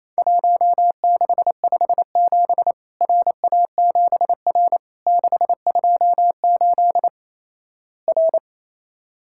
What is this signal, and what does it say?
Morse code, heard as a keyed tone, about 27 wpm, 700 Hz. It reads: 1657 RA7R 628 R
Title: Worked All Europe DX Contest - QTC Exchange - 28wpm